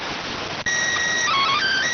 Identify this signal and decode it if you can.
SSTV (slow-scan television), PD 290.